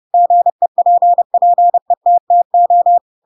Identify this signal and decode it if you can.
Morse code, heard as a keyed tone, about 30 wpm, 700 Hz. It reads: GEPPETTO
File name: Pinocchio / Pinocchio ch3 30 30 ik1hge